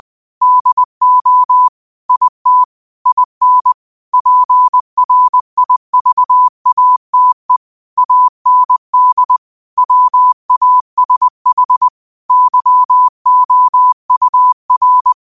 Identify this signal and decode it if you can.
Morse code, heard as a keyed tone, about 20 wpm, 1000 Hz. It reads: DO IT IN PRIVATE AND WASH YOUR
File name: 2025-06-25-20wpm